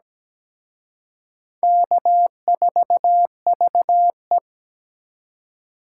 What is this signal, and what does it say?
Morse code, heard as a keyed tone, about 20 wpm, 700 Hz. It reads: K4VE
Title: Speed Racing - USA 1x2 Call Signs 17wpm